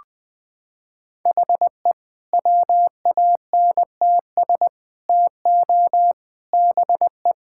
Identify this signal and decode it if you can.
Morse code, heard as a keyed tone, about 20 wpm, 700 Hz. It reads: HE WANTS TO BE
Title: Sentences with 4-Word Phrases - 20wpm